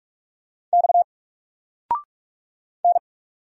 Morse code, heard as a keyed tone, about 44 wpm, 700 Hz. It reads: X N
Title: Single Letter-Number 45wpm